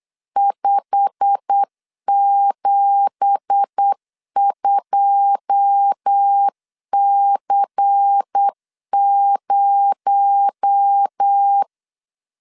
Morse code, heard as a keyed tone, about 8 wpm, 800 Hz. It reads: 572C0
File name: LEZIONI / 10 SIGNS / 10B INT 8wpm 3534